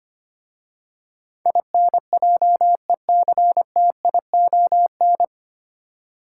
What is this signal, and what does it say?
Morse code, heard as a keyed tone, about 25 wpm, 700 Hz. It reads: INJECTION
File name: practice / 25wpm / 9 Letter Words - First 3 Letters Given - 25wpm